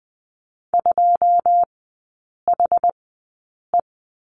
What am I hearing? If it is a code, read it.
Morse code, heard as a keyed tone, about 20 wpm, spaced as for 8 wpm, 700 Hz. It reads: 2HE
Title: 087 Lesson 15e - Number 5 - Three Characters 20wpm